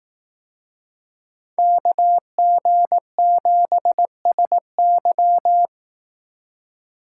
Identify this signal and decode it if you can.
Morse code, heard as a keyed tone, about 18 wpm, 700 Hz. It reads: KG7SY
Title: Speed Racing - USA 2x2 Call Signs 15wpm